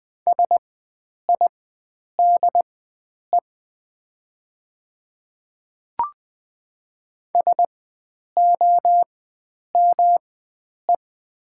Morse code, heard as a keyed tone, about 20 wpm, spaced as for 9 wpm, 690 Hz. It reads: SIDE SOME
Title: 4x Character Spacing - Top 500 Words - 20wpm